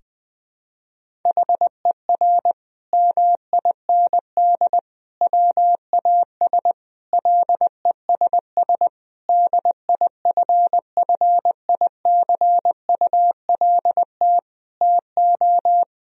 Morse code, heard as a keyed tone, about 21 wpm, 700 Hz. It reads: HER MIND WAS LESS DIFFICULT TO
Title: Sentences from Top 1000 Words - Review Part 2 - 20wpm